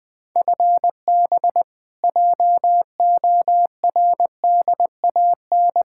Morse code, heard as a keyed tone, about 20 wpm, 700 Hz. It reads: FB JORDAN
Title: Practice QSO - Exchange 3 - More Info - 20wpm